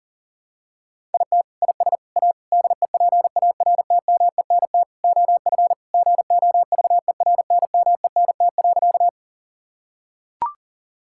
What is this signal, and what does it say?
Morse code, heard as a keyed tone, about 37 wpm, 700 Hz. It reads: IT IS A DEPARTMENT OF GOVERNMENT.
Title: Sentences from Top 1000 Words - Review Part 5 - 40wpm